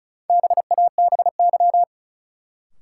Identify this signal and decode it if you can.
Morse code, heard as a keyed tone, about 34 wpm, 700 Hz. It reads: BABY
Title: Sentences from Top 1300 Words - New Word Drill Alt - 35wpm